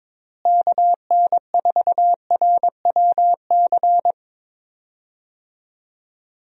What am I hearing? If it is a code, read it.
Morse code, heard as a keyed tone, about 22 wpm, 700 Hz. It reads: KN4RWC
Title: USA Call Signs - 6 Characters - Repeated 2x - 22wpm